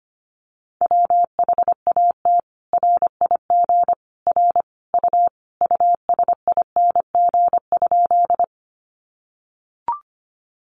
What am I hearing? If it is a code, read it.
Morse code, heard as a keyed tone, about 25 wpm, 700 Hz. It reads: WHAT RIG R U USING?
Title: Common QSO Phrases 25wpm